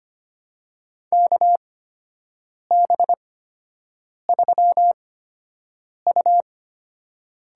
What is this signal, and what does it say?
Morse code, heard as a keyed tone, about 25 wpm, spaced as for 7 wpm, 700 Hz. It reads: KB3U
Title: USA Call Signs - 6 Characters - 8x Character Spacing - 25wpm